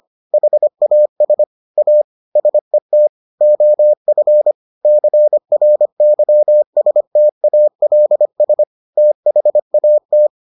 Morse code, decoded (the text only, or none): HAS A SET OF CRYSTALS THAT